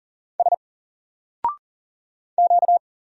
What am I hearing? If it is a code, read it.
Morse code, heard as a keyed tone, about 40 wpm, 700 Hz. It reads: S Q
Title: Single Character (Letter:Number:Punctuation:Prosign) 40wpm